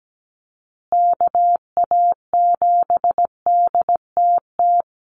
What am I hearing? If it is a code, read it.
Morse code, heard as a keyed tone, about 17 wpm, 700 Hz. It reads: KA7DTT
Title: USA Call Signs Repeated 2x 17wpm